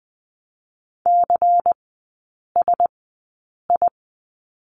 Morse code, heard as a keyed tone, about 20 wpm, spaced as for 8 wpm, 700 Hz. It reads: CSI
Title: 138 Lesson 23e - Letter f - Three Characters 20wpm